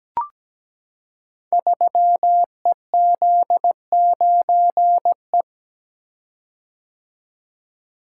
Morse code, heard as a keyed tone, about 17 wpm, 700 Hz. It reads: 3EZ9E
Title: Code Groups - 5 Letter:Numbers - 17wpm